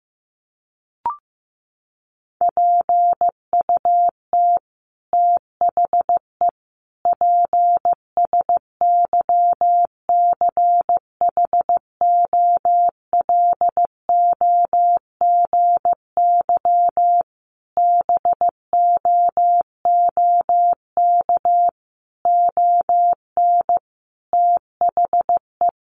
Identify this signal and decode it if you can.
Morse code, heard as a keyed tone, about 15 wpm, 700 Hz. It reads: PUT THE PSYCHOLOGY BOOK ON THE
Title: Sentences from Top 1400 Words 15wpm